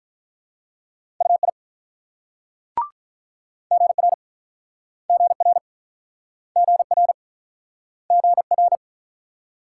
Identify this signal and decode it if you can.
Morse code, heard as a keyed tone, about 50 wpm, 700 Hz. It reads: UI GR GR GR GR